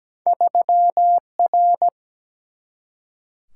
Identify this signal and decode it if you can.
Morse code, heard as a keyed tone, about 17 wpm, 700 Hz. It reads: 3R
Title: Single-Letter Single-Number 17wpm